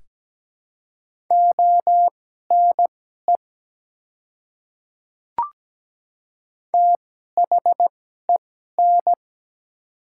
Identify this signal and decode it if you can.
Morse code, heard as a keyed tone, about 17 wpm, spaced as for 12 wpm, 700 Hz. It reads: ONE THEN
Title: 2x Character Spacing - Top 100 Words - 17wpm